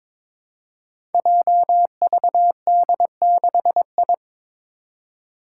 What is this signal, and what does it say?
Morse code, heard as a keyed tone, about 22 wpm, 700 Hz. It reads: JVD6I
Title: Five Challenging Letter-Number Combinations 22wpm